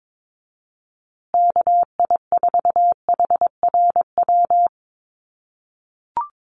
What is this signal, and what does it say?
Morse code, heard as a keyed tone, about 22 wpm, 700 Hz. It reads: KI4HRW
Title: USA Call Signs Repeated 3x 22wpm